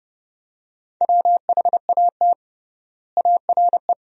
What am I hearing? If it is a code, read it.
Morse code, heard as a keyed tone, about 31 wpm, 700 Hz. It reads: WHAT ARE
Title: 3x Word-Spacing - Sentences from Top 100 Words - 30wpm